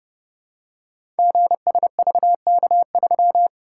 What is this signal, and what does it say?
Morse code, heard as a keyed tone, about 30 wpm, 700 Hz. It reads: GSVK3
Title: Code Groups - 5 Letter:Numbers - 30wpm